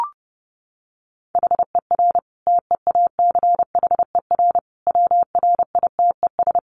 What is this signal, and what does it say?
Morse code, heard as a keyed tone, about 30 wpm, 700 Hz. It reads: HER TEACHER WRITES
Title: Additional Easy Phrases 30wpm